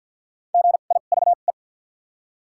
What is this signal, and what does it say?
Morse code, heard as a keyed tone, about 53 wpm, spaced as for 37 wpm, 700 Hz. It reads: GIVE
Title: 2x Character Spacing - Top 300 Words - 50wpm